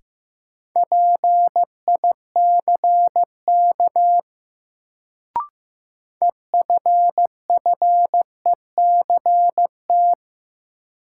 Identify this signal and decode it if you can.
Morse code, heard as a keyed tone, about 15 wpm, 700 Hz. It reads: PICK EFFECT
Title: Sentences from Top 1000 Words - New Word Drill - 15wpm